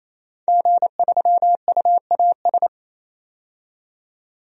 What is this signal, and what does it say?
Morse code, heard as a keyed tone, about 28 wpm, 700 Hz. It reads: G3UAS
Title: DXCC-100 Call Signs - 5 Characters - 28wpm